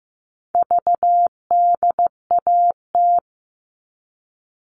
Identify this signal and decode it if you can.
Morse code, heard as a keyed tone, about 15 wpm, 700 Hz. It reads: VDAT
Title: All Cut Numbers - 4-Digit 15wpm